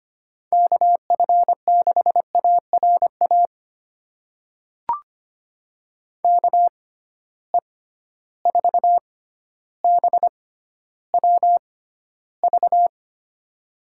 Morse code, heard as a keyed tone, about 25 wpm, spaced as for 13 wpm, 700 Hz. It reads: KF6ARA KE4BWV
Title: USA Call Signs - 6 Characters - 6x Character Spacing - 25wpm